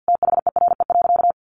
Morse code, heard as a keyed tone, about 48 wpm, 700 Hz. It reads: THERE.
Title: Sentences from Top 1100 Words 50wpm